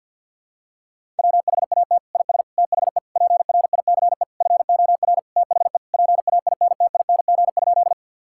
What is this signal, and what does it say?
Morse code, heard as a keyed tone, about 49 wpm, 700 Hz. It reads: WHAT IS THE PRICE FOR THE PAINTING?
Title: Sentences from Top 1000 Words - Review Part 6 - 50wpm